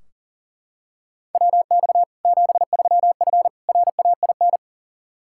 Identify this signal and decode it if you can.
Morse code, heard as a keyed tone, about 40 wpm, 700 Hz. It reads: WX 73F RAIN
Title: Practice QSO - Exchange 2 - Info - 40wpm